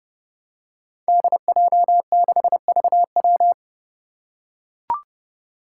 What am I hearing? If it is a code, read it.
Morse code, heard as a keyed tone, about 32 wpm, 700 Hz. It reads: DJ6VW
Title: Five Challenging Letter-Number Combinations 30wpm